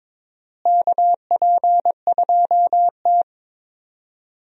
Morse code, heard as a keyed tone, about 22 wpm, 700 Hz. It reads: KP2T
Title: DXCC-100 Call Signs - 4 Characters - Repeated 3x - 22wpm